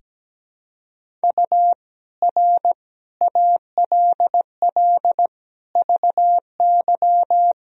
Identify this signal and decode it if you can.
Morse code, heard as a keyed tone, about 17 wpm, 700 Hz. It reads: U R ALL VY
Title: Sentences from Top 200 Words and Abbreviations 17wpm